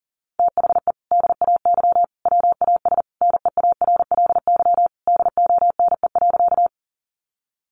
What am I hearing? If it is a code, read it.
Morse code, heard as a keyed tone, about 40 wpm, 700 Hz. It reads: THE DAY WAS NEARLY DONE.
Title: Sentences from Top 1300 Words 40wpm